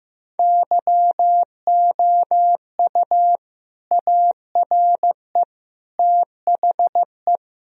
Morse code, heard as a keyed tone, about 15 wpm, 700 Hz. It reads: YOU ARE THE
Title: Sentences from Top 1000 Words - Review Part 2 - 15wpm